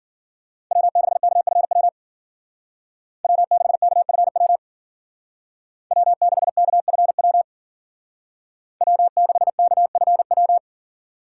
Morse code, heard as a keyed tone, about 49 wpm, 700 Hz. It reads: W6KFW W6KFW W6KFW W6KFW
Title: Speed Racing - USA 1x3 Call Signs 40wpm